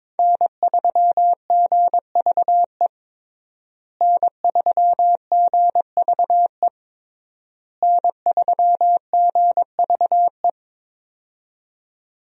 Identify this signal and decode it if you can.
Morse code, heard as a keyed tone, about 21 wpm, 700 Hz. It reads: N3GVE N3GVE N3GVE
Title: USA Call Signs - 5 Characters - Repeated 3x - 22wpm